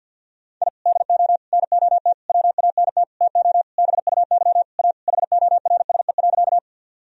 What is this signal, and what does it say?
Morse code, heard as a keyed tone, about 51 wpm, 700 Hz. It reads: I DO NOT WANT TO BUY A HORSE.
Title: Sentences from Top 1000 Words - Review Part 4 - 50wpm